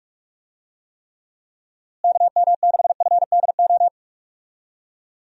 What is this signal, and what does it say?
Morse code, heard as a keyed tone, about 41 wpm, 700 Hz. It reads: KM6FDO
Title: USA Call Signs - 6 Characters - Repeated 2x - 45wpm